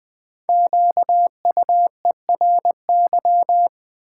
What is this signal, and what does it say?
Morse code, heard as a keyed tone, about 20 wpm, 700 Hz. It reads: QUERY
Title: Build A Word - 1 Letter At A Time - 5-Letter Words - 20wpm